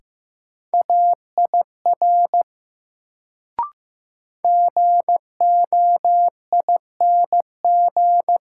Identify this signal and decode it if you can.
Morse code, heard as a keyed tone, about 15 wpm, 700 Hz. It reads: AIR GOING
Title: Sets of 2 Words - New Word Drill - 15wpm